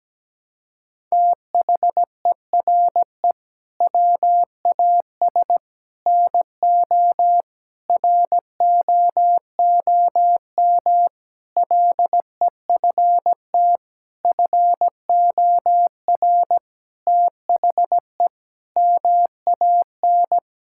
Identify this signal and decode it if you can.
Morse code, heard as a keyed tone, about 17 wpm, 700 Hz. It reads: THERE WAS NO ROOM LEFT FOR THE MAN
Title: Sentences from Top 1600 Words 17wpm